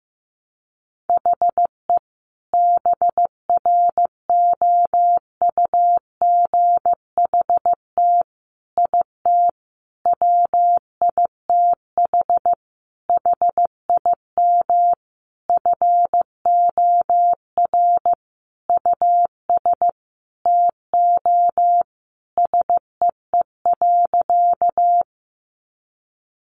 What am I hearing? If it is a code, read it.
Morse code, heard as a keyed tone, about 15 wpm, 700 Hz. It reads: HE BROUGHT IT WITH HIM FOR US TO SEE.
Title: Sentences from Top 500 Words 15wpm